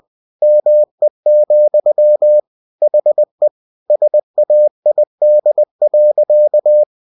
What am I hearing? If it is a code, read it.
Morse code, heard as a keyed tone, about 20 wpm, 600 Hz. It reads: ME, HE SAID.
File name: cw-books / winnie-the-pooh / winnie-20wpm-page-0017